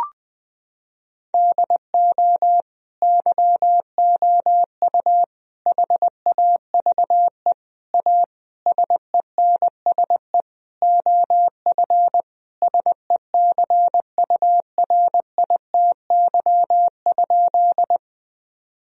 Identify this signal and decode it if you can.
Morse code, heard as a keyed tone, about 20 wpm, 700 Hz. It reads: DO YOU HAVE A SENSE OF SECURITY?
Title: Sentences from Top 1000 Words - Review Part 4 - 20wpm